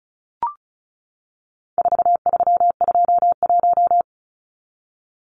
Morse code, heard as a keyed tone, about 34 wpm, 700 Hz. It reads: 4321
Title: Numbers - 4-Digit 35wpm